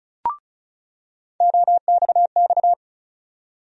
Morse code, heard as a keyed tone, about 34 wpm, 700 Hz. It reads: OXX
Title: USA Call Sign Postfixes - 3 Letters 35wpm